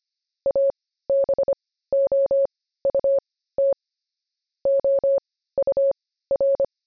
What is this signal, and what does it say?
Morse code, heard as a keyed tone, about 25 wpm, spaced as for 15 wpm, 550 Hz. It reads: ABOUT OUR
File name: jb recaptured / morse